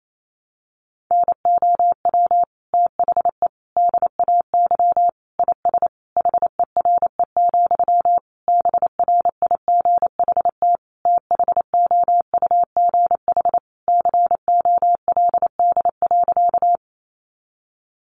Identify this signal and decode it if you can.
Morse code, heard as a keyed tone, about 28 wpm, 700 Hz. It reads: NOW THE DAY IS HERE, BRIGHT THOUGH COLD.
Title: Sentences from Top 1000 Words 28wpm